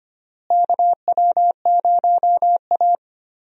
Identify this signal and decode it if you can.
Morse code, heard as a keyed tone, about 25 wpm, 700 Hz. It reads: KW0A
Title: USA Call Signs - 4 Characters - Repeated 3x - 25wpm